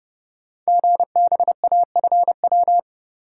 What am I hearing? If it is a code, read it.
Morse code, heard as a keyed tone, about 30 wpm, 700 Hz. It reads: GBAFW
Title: Code Groups - 5 Letters - 30wpm